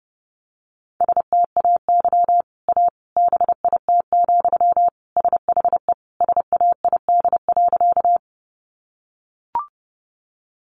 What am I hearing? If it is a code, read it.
Morse code, heard as a keyed tone, about 29 wpm, 700 Hz. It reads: STAY A BIT, SHE SAID.
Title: Sentences from Top 900 Words 30wpm